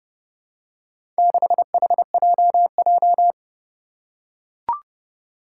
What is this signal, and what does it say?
Morse code, heard as a keyed tone, about 30 wpm, 700 Hz. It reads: 6HJJ
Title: Four Challenging Letter-Number Combinations 30wpm